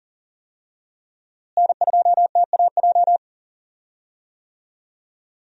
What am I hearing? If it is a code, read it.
Morse code, heard as a keyed tone, about 40 wpm, 700 Hz. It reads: N2TAJ
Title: USA Call Signs - 5 Characters - Repeated 2x - 40wpm